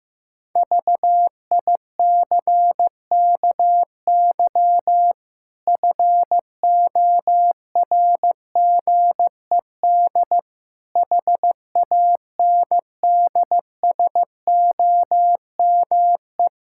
Morse code, heard as a keyed tone, about 16 wpm, 700 Hz. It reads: VICKY FORGED HANDSOME
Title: Pangram Challenge v2 - 15wpm